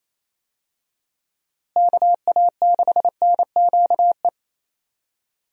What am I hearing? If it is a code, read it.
Morse code, heard as a keyed tone, about 28 wpm, 700 Hz. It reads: KA6NQE
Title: USA Call Signs - 6 Characters - Repeated 3x - 28wpm